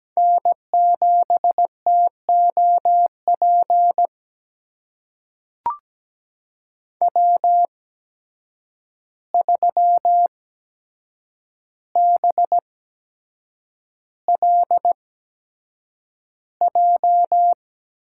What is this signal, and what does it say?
Morse code, heard as a keyed tone, about 17 wpm, spaced as for 7 wpm, 700 Hz. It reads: N7TOP W3BLJ
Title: USA Call Signs - 5 Characters - 8x Character Spacing - 17wpm